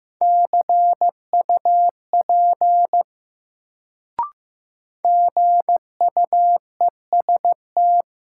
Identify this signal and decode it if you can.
Morse code, heard as a keyed tone, about 15 wpm, 700 Hz. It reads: CUP GUEST
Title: Sentences from Top 1600 Words - New Word Drill - 15wpm